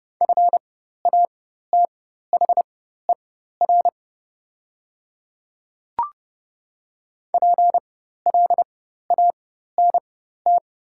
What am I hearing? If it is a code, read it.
Morse code, heard as a keyed tone, about 30 wpm, spaced as for 14 wpm, 700 Hz. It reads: FATHER PLANT
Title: 4x Character Spacing - Top 300 Words - 30wpm